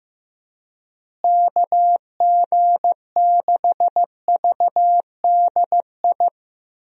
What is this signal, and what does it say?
Morse code, heard as a keyed tone, about 15 wpm, 700 Hz. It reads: KG6VDI
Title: USA Call Signs - 6 Characters - Repeated 3x - 15wpm